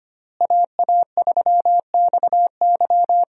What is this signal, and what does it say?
Morse code, heard as a keyed tone, about 25 wpm, 700 Hz. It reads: AA3XY
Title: USA Call Signs - 5 Characters - Repeated 2x - 25wpm